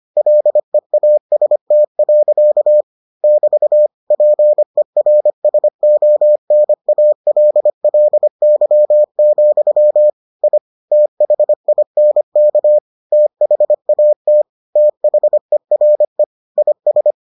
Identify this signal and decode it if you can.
Morse code, heard as a keyed tone, about 25 wpm, 600 Hz. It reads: LEAST. <BT> PERSONALLY, I THINK THAT THERE IS